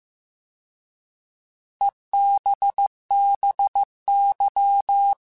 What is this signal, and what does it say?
Morse code, heard as a keyed tone, about 15 wpm, 800 Hz. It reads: EBBY